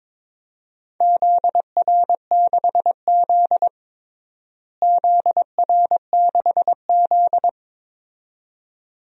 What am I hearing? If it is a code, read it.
Morse code, heard as a keyed tone, about 22 wpm, 700 Hz. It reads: ZR6Z ZR6Z
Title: DXCC-100 Call Signs - 4 Characters - Repeated 2x - 22wpm